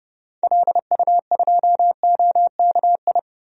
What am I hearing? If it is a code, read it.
Morse code, heard as a keyed tone, about 31 wpm, 700 Hz. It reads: LU2OKI
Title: DXCC-100 Call Signs - 6 Characters - Repeated 3x - 30wpm